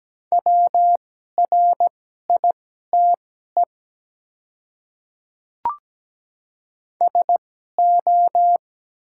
Morse code, heard as a keyed tone, about 17 wpm, spaced as for 12 wpm, 700 Hz. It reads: WRITE SO
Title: 2x Character Spacing - Top 100 Words - 17wpm